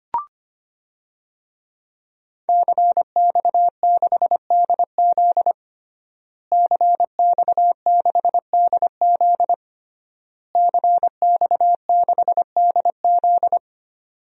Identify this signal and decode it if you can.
Morse code, heard as a keyed tone, about 25 wpm, 700 Hz. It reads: CX6DZ CX6DZ CX6DZ
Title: DXCC-100 Call Signs - 5 Characters - Repeated 3x - 25wpm